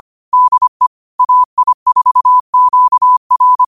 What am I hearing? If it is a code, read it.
Morse code, heard as a keyed tone, about 25 wpm, 1000 Hz. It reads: DE AI4QR